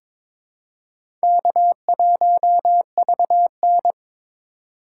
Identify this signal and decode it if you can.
Morse code, heard as a keyed tone, about 22 wpm, 700 Hz. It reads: K1VN
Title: USA Call Signs - 4 Characters - 2x Character Spacing - 22wpm